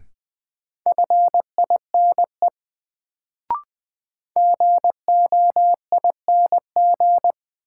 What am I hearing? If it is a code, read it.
Morse code, heard as a keyed tone, about 21 wpm, 700 Hz. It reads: FINE GOING
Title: Sentences from Top 500 Words - New Word Drill - 20wpm